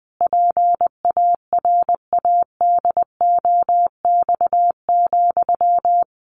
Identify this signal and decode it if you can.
Morse code, heard as a keyed tone, about 20 wpm, 700 Hz. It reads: PARADOX,
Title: A Sound Of Thunder 20wpm